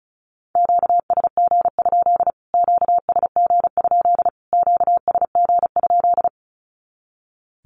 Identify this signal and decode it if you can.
Morse code, heard as a keyed tone, about 34 wpm, 700 Hz. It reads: QSG? QSG? QSG?
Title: Q-Codes Repeated 3x 35wpm